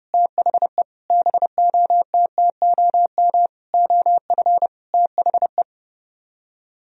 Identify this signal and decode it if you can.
Morse code, heard as a keyed tone, about 31 wpm, 700 Hz. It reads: THE BOTTOM OF THE